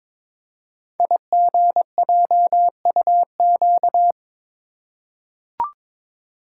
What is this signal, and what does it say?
Morse code, heard as a keyed tone, about 22 wpm, 700 Hz. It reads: IGJUQ
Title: Code Groups - 5 Letters - 22wpm